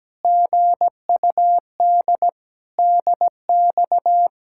Morse code, heard as a keyed tone, about 17 wpm, 700 Hz. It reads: GUD DX
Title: Speed-Racing Common QSO Phrases Encore - 17wpm